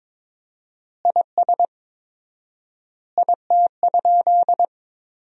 Morse code, heard as a keyed tone, about 23 wpm, 700 Hz. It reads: IS IT?
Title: 4x Word-Spacing - Sentences from Top 100 Words - 22wpm